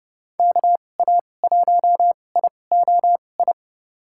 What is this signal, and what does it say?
Morse code, heard as a keyed tone, about 30 wpm, spaced as for 22 wpm, 700 Hz. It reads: KA1IOI
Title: USA Call Signs - 6 Characters - 2x Character Spacing - 30wpm